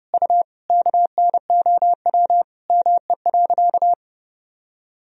Morse code, heard as a keyed tone, about 30 wpm, 700 Hz. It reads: U KNOW ME.
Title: Sentences from Top 200 Words and Abbreviations 30wpm